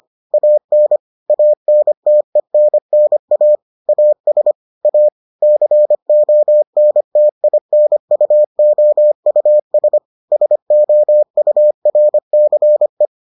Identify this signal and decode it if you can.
Morse code, heard as a keyed tone, about 25 wpm, 600 Hz. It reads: AN ANTENNA AS A CONTINUOUS SOURCE